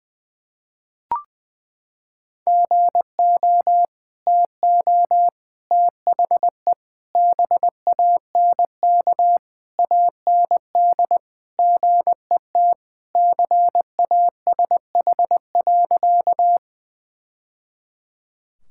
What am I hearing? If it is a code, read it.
Morse code, heard as a keyed tone, about 21 wpm, 700 Hz. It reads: GO TO THE BANK AND GET CASH.
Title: Sentences from Top 1000 Words - Review Part 5 - 20wpm